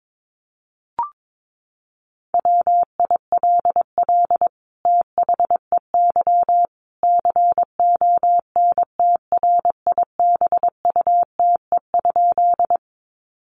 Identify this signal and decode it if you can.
Morse code, heard as a keyed tone, about 22 wpm, 700 Hz. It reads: WILL THEY CONTRIBUTE?
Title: Sentences from Top 1400 Words 22wpm